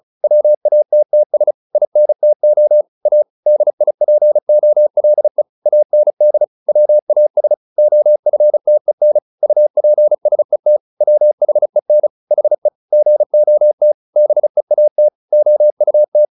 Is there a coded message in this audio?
Yes: Morse code from 0.2 to 16.2 s, heard as a keyed tone, about 34 wpm, 600 Hz. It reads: WATTS INTO A DIPOLE AND WAS OFTEN UPSET WHEN HE GOT BEAT OUT